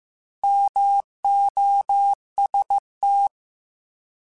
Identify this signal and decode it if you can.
Morse code, heard as a keyed tone, about 15 wpm, 800 Hz. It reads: MOST
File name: Alphabetical / Alphabetical mp3s / Ms @13